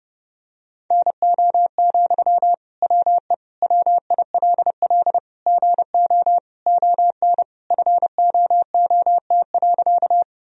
Morse code, heard as a keyed tone, about 30 wpm, 700 Hz. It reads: NO, WE WILL GO ON FOOT.
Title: Sentences from Top 1000 Words - Review Part 3 - 30wpm